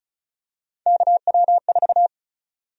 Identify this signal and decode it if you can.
Morse code, heard as a keyed tone, about 36 wpm, 700 Hz. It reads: KW4
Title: USA Call Sign Prefixes - 35wpm